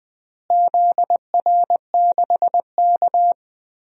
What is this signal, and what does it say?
Morse code, heard as a keyed tone, about 20 wpm, 700 Hz. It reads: ZR6K
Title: DXCC-100 Call Signs - 4 Characters - Repeated 3x - 20wpm